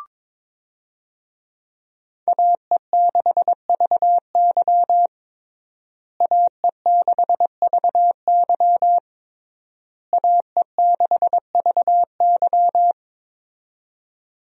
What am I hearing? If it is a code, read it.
Morse code, heard as a keyed tone, about 22 wpm, 700 Hz. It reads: AE6VY AE6VY AE6VY
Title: USA Call Signs - 5 Characters - Repeated 3x - 22wpm